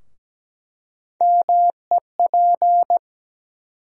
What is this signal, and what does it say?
Morse code, heard as a keyed tone, about 17 wpm, 700 Hz. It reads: MEP
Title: USA Call Sign Postfixes - 3 Letters 17wpm